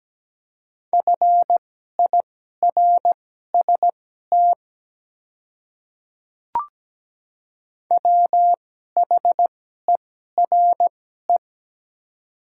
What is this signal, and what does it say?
Morse code, heard as a keyed tone, about 17 wpm, spaced as for 12 wpm, 700 Hz. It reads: FIRST WHERE
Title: 2x Character Spacing - Top 200 Words - 17wpm